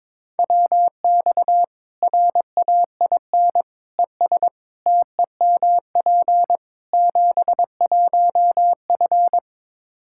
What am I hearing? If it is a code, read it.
Morse code, heard as a keyed tone, about 21 wpm, 700 Hz. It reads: WX RAIN ES TEMP 71F
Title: K1USN SST QSO Practice 22wpm